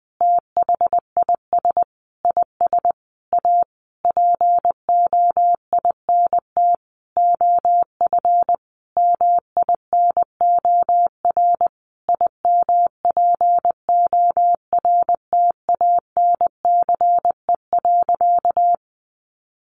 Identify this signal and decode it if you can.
Morse code, heard as a keyed tone, about 20 wpm, 700 Hz. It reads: THIS IS A POINT OF MINOR IMPORTANCE.
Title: Sentences from Top 1500 Words 20wpm